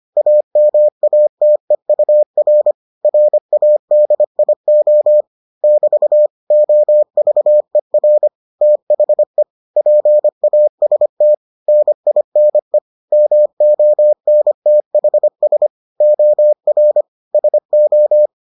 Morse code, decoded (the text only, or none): AMATEUR RADIO <BT> OVER THE PAST NINE MONTHS OR SO